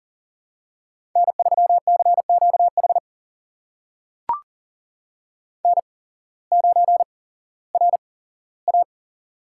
Morse code, heard as a keyed tone, about 40 wpm, spaced as for 19 wpm, 700 Hz. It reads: N3CQH N9RA
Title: USA Call Signs - 8x Character Spacing - 40wpm